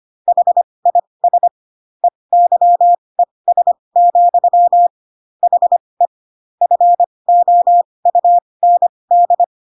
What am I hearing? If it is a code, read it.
Morse code, heard as a keyed tone, about 25 wpm, spaced as for 20 wpm, 700 Hz. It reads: HIS EYES, HE FOUND
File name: Pinocchio / Pinocchio ch1 15 25 ik1hge